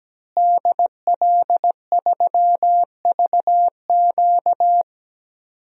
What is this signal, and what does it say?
Morse code, heard as a keyed tone, about 17 wpm, 700 Hz. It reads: DL3VQ
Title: DXCC-100 Call Signs - 5 Characters - Repeated 2x - 17wpm